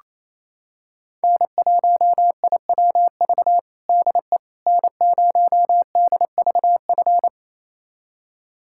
Morse code, heard as a keyed tone, about 27 wpm, 700 Hz. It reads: N1IWV DE N0DVF
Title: USA Callsign DE Callsign - 5 Characters - 28wpm